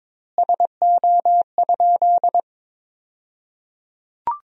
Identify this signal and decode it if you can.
Morse code, heard as a keyed tone, about 22 wpm, 700 Hz. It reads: SO?
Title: 2x Word-Spacing - Sentences from Top 200 Words - 22wpm